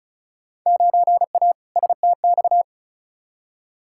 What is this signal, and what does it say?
Morse code, heard as a keyed tone, about 34 wpm, 700 Hz. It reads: 9A STX
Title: Field Day Exchanges - 35wpm